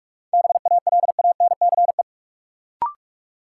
Morse code, heard as a keyed tone, about 46 wpm, 700 Hz. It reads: BALANCE
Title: Sentences from Top 900 Words - New Word Drill - 45wpm